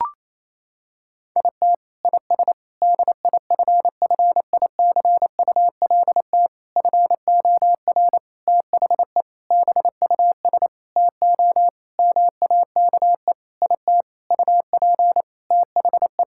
Morse code, heard as a keyed tone, about 28 wpm, 700 Hz. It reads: IT IS DIFFICULT FOR THE BUS TO MAKE IT UP THE
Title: Sentences from Top 1000 Words - Review Part 6 - 28wpm